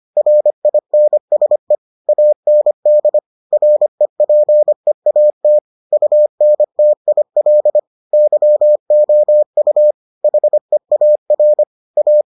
Morse code, heard as a keyed tone, about 25 wpm, 600 Hz. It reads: RINSE AND REPEAT UNTIL YOU HEAR A